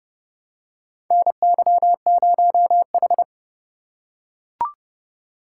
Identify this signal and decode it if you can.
Morse code, heard as a keyed tone, about 30 wpm, 700 Hz. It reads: NY0H
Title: USA Call Signs - 4 Characters - 4x Character Spacing - 30wpm